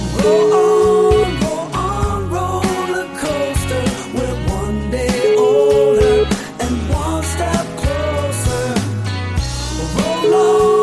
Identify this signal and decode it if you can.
Ringback tone (Europe, 425 Hz), with noise.